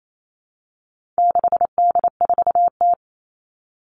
Morse code, heard as a keyed tone, about 28 wpm, 700 Hz. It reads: 6D4T
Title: All Cut Numbers - 4-Digit 28wpm